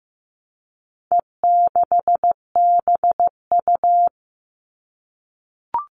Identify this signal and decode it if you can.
Morse code, heard as a keyed tone, about 15 wpm, 700 Hz. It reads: E6BU
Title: All Cut Numbers - 4-Digit 15wpm